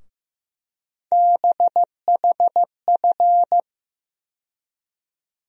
Morse code, heard as a keyed tone, about 15 wpm, 700 Hz. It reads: BHF
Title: USA Call Sign Postfixes - 3 Letters 15wpm